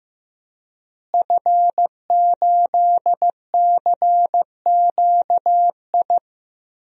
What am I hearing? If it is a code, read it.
Morse code, heard as a keyed tone, about 15 wpm, 700 Hz. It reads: F8CQI
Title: DXCC-100 Call Signs - 5 Characters - Repeated 3x - 15wpm